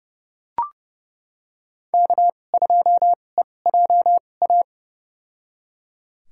Morse code, heard as a keyed tone, about 30 wpm, spaced as for 22 wpm, 700 Hz. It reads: K2EJA
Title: USA Call Signs - 5 Characters - 2x Character Spacing - 30wpm